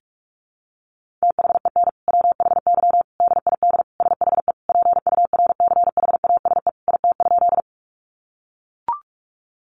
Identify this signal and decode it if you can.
Morse code, heard as a keyed tone, about 44 wpm, 700 Hz. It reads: THEN WHY DID SHE PURCHASE IT?
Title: Sentences from Top 800 Words 45wpm